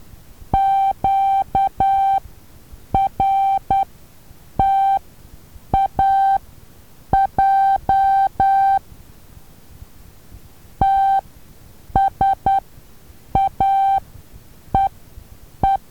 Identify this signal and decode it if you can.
Morse code, heard as a keyed tone, about 10 wpm, spaced as for 7 wpm, 780 Hz. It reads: QRTAJ TSAEE
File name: Taviro HA6VA B